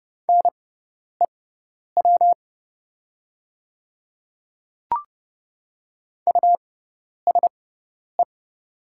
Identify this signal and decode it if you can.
Morse code, heard as a keyed tone, about 30 wpm, spaced as for 10 wpm, 700 Hz. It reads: NEW USE
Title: 6x Character Spacing - Top 300 Words - 30wpm